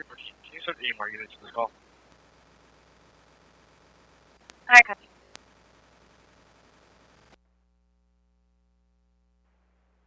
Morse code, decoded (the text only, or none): none